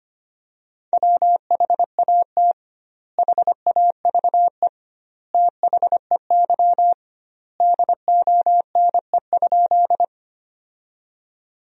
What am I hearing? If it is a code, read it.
Morse code, heard as a keyed tone, about 25 wpm, 700 Hz. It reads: WHAT HAVE THEY DONE?
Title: 2x Word-Spacing - Sentences from Top 500 Words - 25wpm